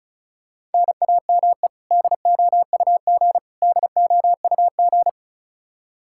Morse code, heard as a keyed tone, about 34 wpm, 700 Hz. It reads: NAME DOUG DOUG
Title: Practice QSO - Exchange 1 - Beginning - 35wpm